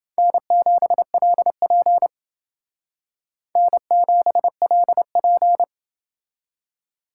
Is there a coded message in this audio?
Yes: Morse code from 0.2 to 5.6 s, heard as a keyed tone, about 30 wpm, 700 Hz. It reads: N7LP N7LP